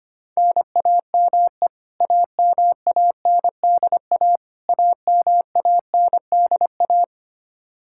Morse code, heard as a keyed tone, about 25 wpm, 700 Hz. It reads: NAME AMANDA AMANDA
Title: K1USN SST QSO Practice 25wpm